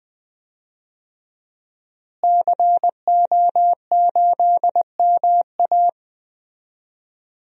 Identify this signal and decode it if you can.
Morse code, heard as a keyed tone, about 20 wpm, 700 Hz. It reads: CO8MA
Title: DXCC-100 Call Signs - 5 Characters - Repeated 2x - 20wpm